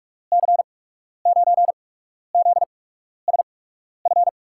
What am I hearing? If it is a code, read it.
Morse code, heard as a keyed tone, about 46 wpm, spaced as for 12 wpm, 700 Hz. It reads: C9ZSF
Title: USA Call Signs - 6 Characters - 8x Character Spacing - 45wpm